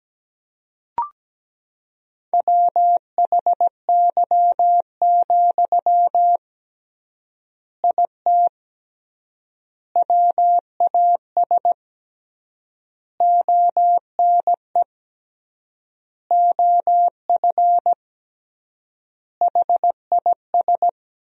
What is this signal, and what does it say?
Morse code, heard as a keyed tone, about 17 wpm, 700 Hz. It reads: WHY, IT WAS ONE OF HIS
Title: 3x Word-Spacing - Sentences from Top 300 Words - 17wpm